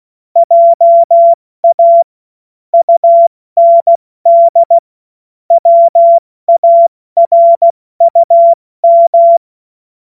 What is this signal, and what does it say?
Morse code, heard as a keyed tone, about 16 wpm, 680 Hz. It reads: JA UND WARUM
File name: Witz-12-August-2025-70-F80